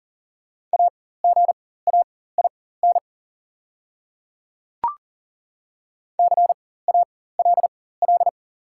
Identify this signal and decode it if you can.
Morse code, heard as a keyed tone, about 40 wpm, spaced as for 19 wpm, 690 Hz. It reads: AGAIN CALL
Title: 4x Character Spacing - Top 500 Words - 40wpm